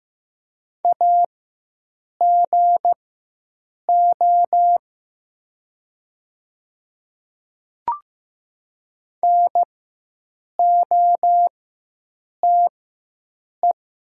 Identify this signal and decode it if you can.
Morse code, heard as a keyed tone, about 15 wpm, spaced as for 7 wpm, 700 Hz. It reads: AGO NOTE
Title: 4x Character Spacing - Top 500 Words - 15wpm